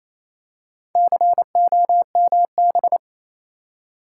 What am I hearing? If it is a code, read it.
Morse code, heard as a keyed tone, about 27 wpm, 690 Hz. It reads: COMB
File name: Easy Words - Speed-Racing with Hints - 22wpm